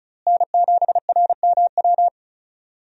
Morse code, heard as a keyed tone, about 34 wpm, 700 Hz. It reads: N7RMW
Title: USA Call Signs - 5 Characters - Repeated 3x - 35wpm